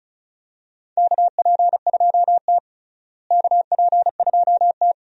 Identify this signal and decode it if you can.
Morse code, heard as a keyed tone, about 36 wpm, 700 Hz. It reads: KP2T KP2T
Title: DXCC-100 Call Signs - 4 Characters - Repeated 3x - 35wpm